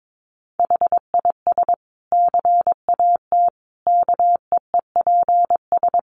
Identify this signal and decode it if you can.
Morse code, heard as a keyed tone, about 22 wpm, 700 Hz. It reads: HIS CAT KEEPS
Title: Additional Easy Phrases 22wpm